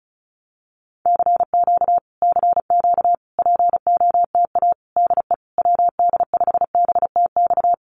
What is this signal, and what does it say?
Morse code, heard as a keyed tone, about 36 wpm, 700 Hz. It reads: CQ CQ POTA DE WD5BTX
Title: Common QSO Phrases Encore 35wpm